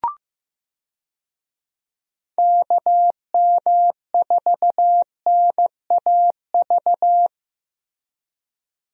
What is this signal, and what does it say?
Morse code, heard as a keyed tone, about 15 wpm, 700 Hz. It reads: KM4NAV
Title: USA Call Signs - 6 Characters - Repeated 3x - 15wpm